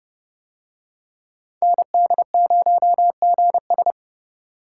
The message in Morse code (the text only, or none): ND0GS